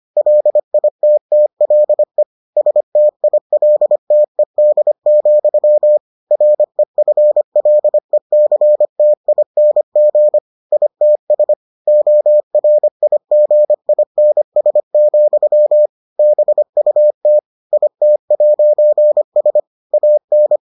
Morse code, heard as a keyed tone, about 25 wpm, 600 Hz. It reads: LITTLE STILTED, REFLECTING ITS ORIGINS, BUT IT'S AN